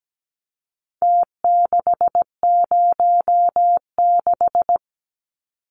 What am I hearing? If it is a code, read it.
Morse code, heard as a keyed tone, about 17 wpm, 700 Hz. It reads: T606
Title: Numbers - Contest Style - 4-Digit 17wpm